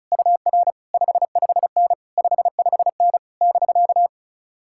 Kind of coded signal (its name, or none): Morse code